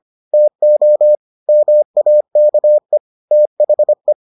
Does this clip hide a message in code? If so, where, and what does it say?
Yes: Morse code from 0.3 to 4.1 s, heard as a keyed tone, about 25 wpm, 600 Hz. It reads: TO MAKE THE